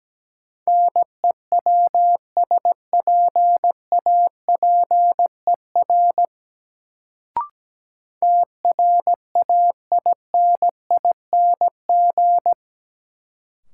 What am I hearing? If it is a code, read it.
Morse code, heard as a keyed tone, about 17 wpm, 700 Hz. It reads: NEWSPAPER TRAINING
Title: Sentences from Top 1000 Words - Review Part 4 - New Word Drill - 17wpm